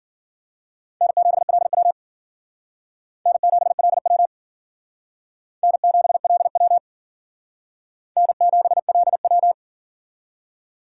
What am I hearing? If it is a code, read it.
Morse code, heard as a keyed tone, about 50 wpm, 700 Hz. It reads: N7LW N7LW N7LW N7LW
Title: Speed Racing - USA 1x2 Call Signs 40wpm